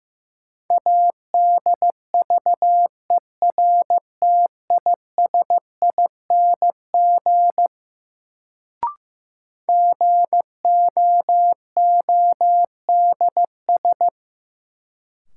Morse code, heard as a keyed tone, about 15 wpm, 700 Hz. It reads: ADVERTISING GOODS
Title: Sentences from Top 1000 Words - Review Part 6 - New Word Drill - 15wpm